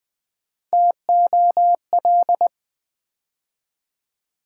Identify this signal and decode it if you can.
Morse code, heard as a keyed tone, about 21 wpm, 700 Hz. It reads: TOL
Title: Build A Word - 3 Letters At A Time - 8-Letter Words - 20wpm